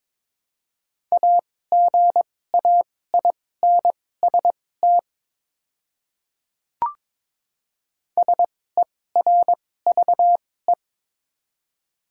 Morse code, heard as a keyed tone, about 22 wpm, spaced as for 16 wpm, 700 Hz. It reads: AGAINST SERVE
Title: 2x Character Spacing - Top 500 Words - 22wpm